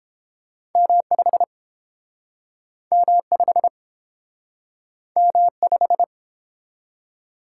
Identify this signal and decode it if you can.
Morse code, heard as a keyed tone, about 30 wpm, 700 Hz. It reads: M5 M5 M5